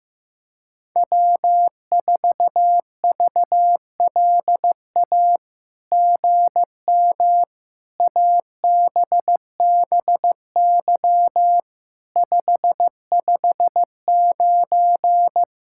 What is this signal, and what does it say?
Morse code, heard as a keyed tone, about 15 wpm, 700 Hz. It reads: W4VLA GM ABBY 559
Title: SOTA QSO Practice 15wpm